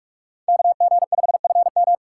Morse code, heard as a keyed tone, about 41 wpm, 700 Hz. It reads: KG5FM